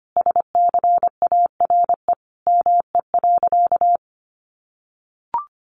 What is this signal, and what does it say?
Morse code, heard as a keyed tone, about 25 wpm, 700 Hz. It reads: SCARE ME.
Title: A Sound Of Thunder 25wpm